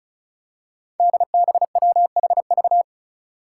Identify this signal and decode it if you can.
Morse code, heard as a keyed tone, about 34 wpm, 700 Hz. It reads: DBWHV
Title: Five Challenging Letter-Number Combinations 35wpm